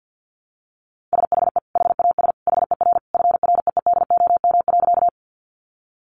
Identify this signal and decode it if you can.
Morse code, heard as a keyed tone, about 50 wpm, 700 Hz. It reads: SHE HAS HER FREEDOM.
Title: Sentences from Top 800 Words 50wpm